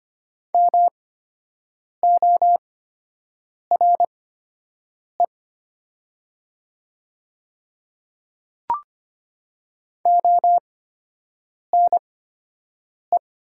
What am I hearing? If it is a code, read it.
Morse code, heard as a keyed tone, about 25 wpm, spaced as for 7 wpm, 700 Hz. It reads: MORE ONE
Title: 8x Character Spacing - Top 200 Words - 25wpm